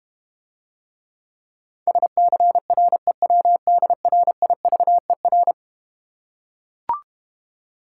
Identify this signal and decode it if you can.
Morse code, heard as a keyed tone, about 33 wpm, 700 Hz. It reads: SCREWDRIVER